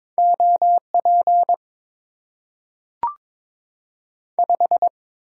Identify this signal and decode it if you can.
Morse code, heard as a keyed tone, about 22 wpm, 700 Hz. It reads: OP 5
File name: Top 90 QSO Elements - 8x Character Spacing - 22wpm